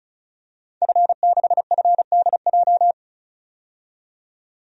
Morse code, heard as a keyed tone, about 36 wpm, 700 Hz. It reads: F6FDJ
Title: DXCC-100 Call Signs - 5 Characters - 35wpm